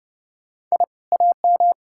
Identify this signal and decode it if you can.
Morse code, heard as a keyed tone, about 38 wpm, 700 Hz. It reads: I AM
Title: Speed-Racing Common QSO Phrases - 30wpm